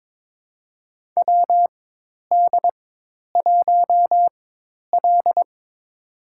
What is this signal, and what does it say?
Morse code, heard as a keyed tone, about 22 wpm, spaced as for 10 wpm, 700 Hz. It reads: WD1L
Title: USA Call Signs - 4 Characters - 4x Character Spacing - 22wpm